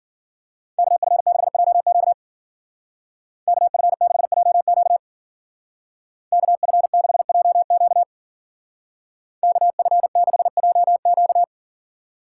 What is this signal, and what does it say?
Morse code, heard as a keyed tone, about 51 wpm, 700 Hz. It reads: KF6JQ KF6JQ KF6JQ KF6JQ